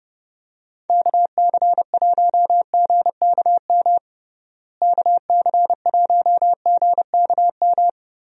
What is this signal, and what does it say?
Morse code, heard as a keyed tone, about 30 wpm, 700 Hz. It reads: KC1GKM KC1GKM
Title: USA Call Signs - 6 Characters - Repeated 2x - 30wpm